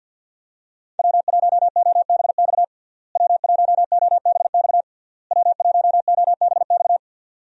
Morse code, heard as a keyed tone, about 50 wpm, 700 Hz. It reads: W1OBX W1OBX W1OBX
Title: USA Call Signs - 5 Characters - Repeated 3x - 50wpm